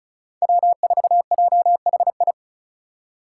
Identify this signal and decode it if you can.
Morse code, heard as a keyed tone, about 36 wpm, 700 Hz. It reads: W4JHI
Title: Five Challenging Letter-Number Combinations 35wpm